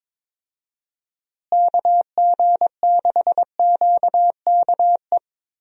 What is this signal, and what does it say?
Morse code, heard as a keyed tone, about 22 wpm, 700 Hz. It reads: KG6QKE
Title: USA Call Signs - 6 Characters - Repeated 2x - 22wpm